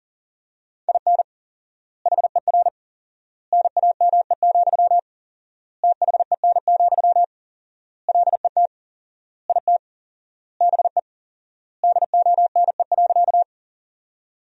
Morse code, heard as a keyed tone, about 37 wpm, 700 Hz. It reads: IN HER NAME, THEN, LET IT BE DONE.
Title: 4x Word-Spacing - Sentences from Top 500 Words - 40wpm